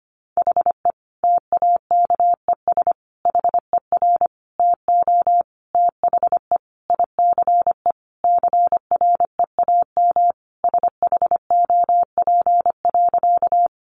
Morse code, heard as a keyed tone, about 25 wpm, 700 Hz. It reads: HE TAKES HER TO THE ICE CREAM SHOP.
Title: Additional Easy Phrases 25wpm